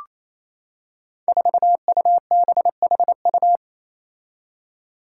Morse code, heard as a keyed tone, about 27 wpm, 700 Hz. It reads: 4UBHU
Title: Five Challenging Letter-Number Combinations 28wpm